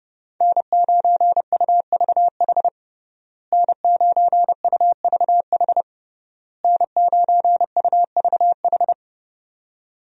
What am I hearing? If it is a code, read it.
Morse code, heard as a keyed tone, about 30 wpm, 700 Hz. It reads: N9UVH N9UVH N9UVH
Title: USA Call Signs - 5 Characters - Repeated 3x - 30wpm